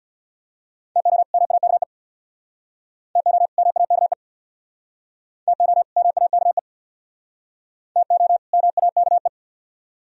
Morse code, heard as a keyed tone, about 67 wpm, 700 Hz. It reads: TO MAKE TO MAKE TO MAKE TO MAKE